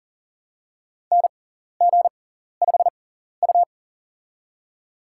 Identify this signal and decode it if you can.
Morse code, heard as a keyed tone, about 43 wpm, spaced as for 14 wpm, 690 Hz. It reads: NG5U
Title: USA Call Signs - 4 Characters - 6x Character Spacing - 40wpm